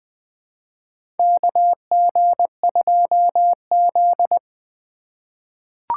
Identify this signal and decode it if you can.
Morse code, heard as a keyed tone, about 20 wpm, 700 Hz. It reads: KG2Z
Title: USA Call Signs - 4 Characters - 2x Character Spacing - 20wpm